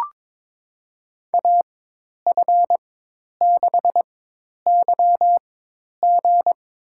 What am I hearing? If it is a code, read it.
Morse code, heard as a keyed tone, about 22 wpm, spaced as for 10 wpm, 700 Hz. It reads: AF6YG
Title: USA Call Signs - 4x Character Spacing - 22wpm